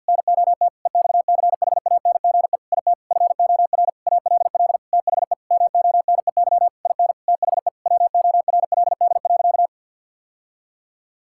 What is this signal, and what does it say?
Morse code, heard as a keyed tone, about 50 wpm, 700 Hz. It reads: NOT EXCHANGE IT FOR ALL THE MONEY IN THE WORLD.
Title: Sentences from Top 1000 Words - Review Part 6 - 50wpm